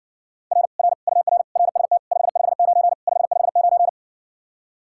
Morse code, heard as a keyed tone, about 61 wpm, 700 Hz. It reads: R R UR RST 559 559